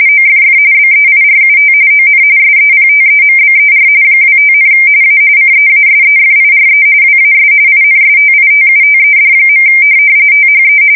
RTTY, about 45.45 baud. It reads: OFF BEIRUT ON SEPTEMBER 25, A CEASEFIRE WAS INSTITUTED. THE
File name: fsk170 / USS-New-Jersey